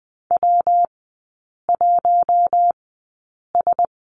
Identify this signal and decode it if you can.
Morse code, heard as a keyed tone, about 20 wpm, spaced as for 8 wpm, 700 Hz. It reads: W1S
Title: 114 Lesson 19e - Letter w - Three Characters 20wpm